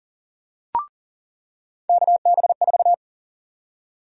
Morse code, heard as a keyed tone, about 40 wpm, 700 Hz. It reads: KB4
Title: USA Call Sign Prefixes - 40wpm